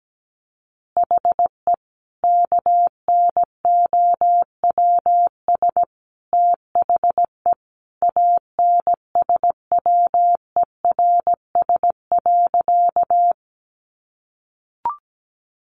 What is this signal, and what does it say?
Morse code, heard as a keyed tone, about 17 wpm, 700 Hz. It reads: HE KNOWS THE ANSWERS.
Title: Additional Easy Phrases 17wpm